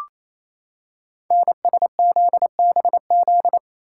Morse code, heard as a keyed tone, about 28 wpm, 700 Hz. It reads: NSZBZ
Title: Code Groups - 5 Letters - 28wpm